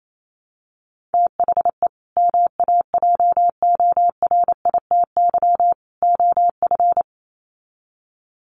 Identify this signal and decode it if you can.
Morse code, heard as a keyed tone, about 28 wpm, 700 Hz. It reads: THE MAJORITY OF
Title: Sets of 3 Words 28wpm